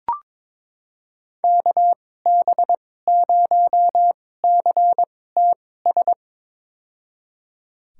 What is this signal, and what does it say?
Morse code, heard as a keyed tone, about 22 wpm, spaced as for 16 wpm, 700 Hz. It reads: KB0CTS
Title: USA Call Signs - 6 Characters - 2x Character Spacing - 22wpm